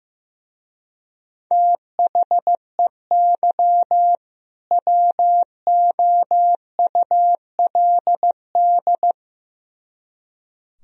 Morse code, heard as a keyed tone, about 17 wpm, 700 Hz. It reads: THEY WOULD
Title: Speed-Racing - Sets of 2 Words 15wpm